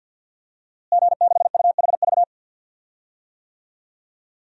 Morse code, heard as a keyed tone, about 46 wpm, 700 Hz. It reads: G6UHV
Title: DXCC-100 Call Signs - 5 Characters - Repeated 3x - 50wpm